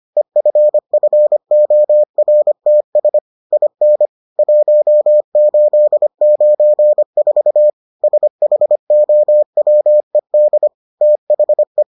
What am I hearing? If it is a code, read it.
Morse code, heard as a keyed tone, about 25 wpm, 600 Hz. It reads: EFFORTS IN 1894 SHOWED THE